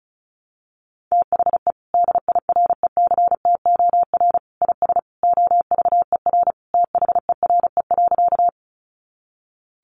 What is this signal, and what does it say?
Morse code, heard as a keyed tone, about 34 wpm, 700 Hz. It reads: THE DIRECTOR IS OVER THERE.
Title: Sentences from Top 1300 Words 35wpm